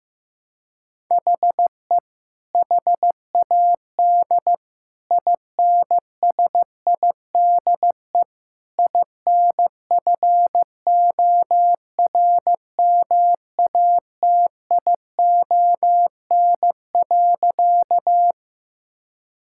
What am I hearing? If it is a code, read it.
Morse code, heard as a keyed tone, about 16 wpm, 700 Hz. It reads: HE HAD INSIDE INFORMATION.
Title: Sentences from Top 1000 Words - Review Part 6 - 15wpm